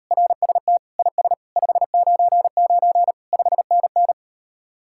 Morse code, heard as a keyed tone, about 38 wpm, 700 Hz. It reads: RST IS 599 5NN